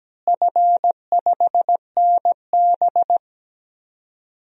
Morse code, heard as a keyed tone, about 17 wpm, 700 Hz. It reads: F5NB
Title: DXCC-100 Call Signs - 4 Characters - Repeated 2x - 17wpm